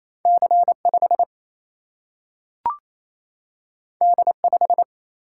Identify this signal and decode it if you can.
Morse code, heard as a keyed tone, about 28 wpm, 700 Hz. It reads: C5 D5
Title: Single-Letter Single-Number 28wpm